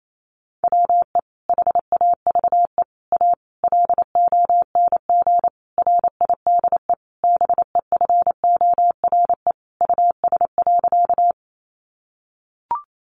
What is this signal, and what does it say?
Morse code, heard as a keyed tone, about 28 wpm, 700 Hz. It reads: WE HAVE A LONG RIDE BEFORE US.
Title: Sentences from Top 1200 Words 28wpm